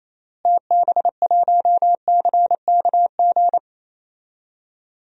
Morse code, heard as a keyed tone, about 28 wpm, 700 Hz. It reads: TB1CKG
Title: DXCC-100 Call Signs - 6 Characters - Repeated 3x - 28wpm